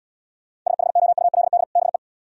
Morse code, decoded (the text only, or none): SHOULD BE